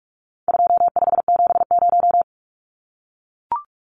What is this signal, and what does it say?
Morse code, heard as a keyed tone, about 44 wpm, 700 Hz. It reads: 2570
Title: Numbers - 4-Digit 45wpm